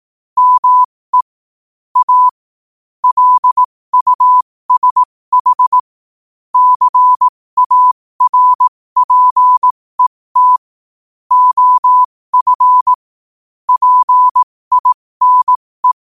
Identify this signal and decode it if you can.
Morse code, heard as a keyed tone, about 18 wpm, spaced as for 15 wpm, 1000 Hz. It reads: ME A LUSH CARPET OF PINE
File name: qotdfiles / 15wpm